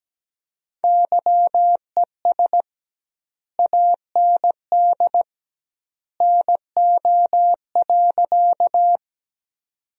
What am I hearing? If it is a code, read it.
Morse code, heard as a keyed tone, about 17 wpm, 700 Hz. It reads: YES AND NO.
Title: 2x Word-Spacing - Sentences from Top 500 Words - 17wpm